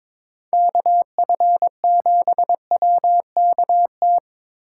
Morse code, heard as a keyed tone, about 26 wpm, 700 Hz. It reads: KF7WKT
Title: Speed Racing - USA Call Signs 22wpm